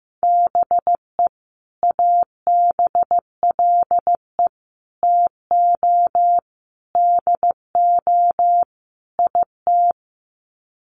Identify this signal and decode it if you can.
Morse code, heard as a keyed tone, about 15 wpm, 700 Hz. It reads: BE ABLE TO DO IT
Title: Sets of 5 Words Encore2 15wpm